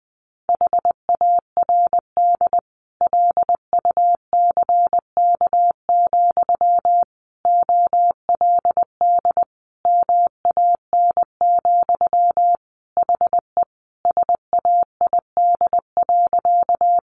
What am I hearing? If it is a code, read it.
Morse code, heard as a keyed tone, about 20 wpm, 700 Hz. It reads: HARD LUCK, OLD MAN, HE SAID.
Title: Sentences from Top 1500 Words 20wpm